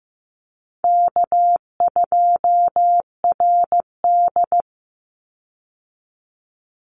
Morse code, heard as a keyed tone, about 15 wpm, 700 Hz. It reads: K2RD
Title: Easy Call Signs Repeated 2x - 15wpm